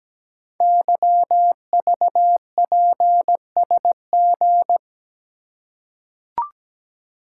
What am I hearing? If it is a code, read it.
Morse code, heard as a keyed tone, about 17 wpm, 700 Hz. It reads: YVPSG
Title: Code Groups - 5 Letters - 17wpm